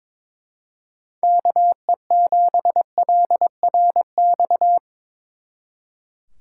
Morse code, heard as a keyed tone, about 24 wpm, 700 Hz. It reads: KE7LRX
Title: Speed Racing - USA 2x3 Call Signs 22wpm